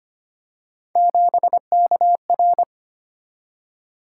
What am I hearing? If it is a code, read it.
Morse code, heard as a keyed tone, about 29 wpm, 700 Hz. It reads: 7KR
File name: Speed-Racing - Three Letter-Number Combinations - 25wpm